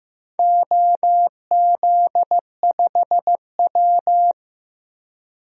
Morse code, heard as a keyed tone, about 15 wpm, 700 Hz. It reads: OZ5W
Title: DXCC-100 Call Signs - 4 Characters - Repeated 3x - 15wpm